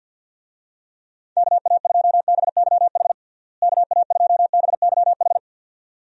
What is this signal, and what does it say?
Morse code, heard as a keyed tone, about 51 wpm, 700 Hz. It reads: KA2BYH KA2BYH
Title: USA Call Signs - 6 Characters - Repeated 2x - 50wpm